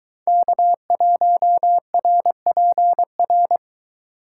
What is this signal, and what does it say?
Morse code, heard as a keyed tone, about 22 wpm, 700 Hz. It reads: K1RPR